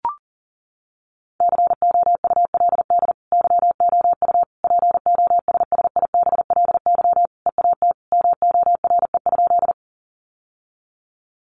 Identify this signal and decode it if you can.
Morse code, heard as a keyed tone, about 40 wpm, 700 Hz. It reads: COULD YOU POSSIBLY EAT MORE?
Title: Sentences from Top 1300 Words 40wpm